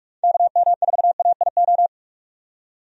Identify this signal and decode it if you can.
Morse code, heard as a keyed tone, about 46 wpm, 700 Hz. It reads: KM4AIO
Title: USA Call Signs - 6 Characters - 45wpm